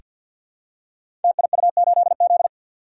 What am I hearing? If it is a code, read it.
Morse code, heard as a keyed tone, about 46 wpm, 700 Hz. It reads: TIU8Z
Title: Code Groups - 5 Letter:Numbers - 50wpm